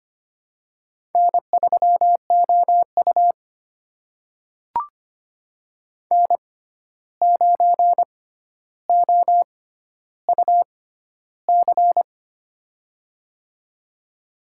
Morse code, heard as a keyed tone, about 25 wpm, spaced as for 12 wpm, 700 Hz. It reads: N3OU N9OUC
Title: USA Call Signs - 6x Character Spacing - 25wpm